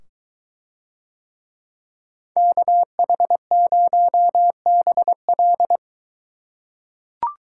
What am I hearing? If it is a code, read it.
Morse code, heard as a keyed tone, about 23 wpm, 700 Hz. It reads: KH0BL